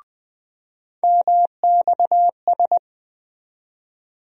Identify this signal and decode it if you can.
Morse code, heard as a keyed tone, about 20 wpm, 700 Hz. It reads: MXS
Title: USA Call Sign Postfixes - 3 Letters 20wpm